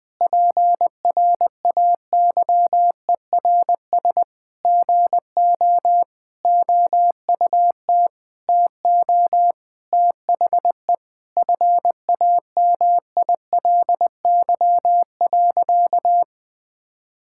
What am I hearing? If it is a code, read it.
Morse code, heard as a keyed tone, about 21 wpm, 700 Hz. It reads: PRAYERS GO OUT TO THE FAMILY.
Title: Sentences with 8-Word Phrases - 20wpm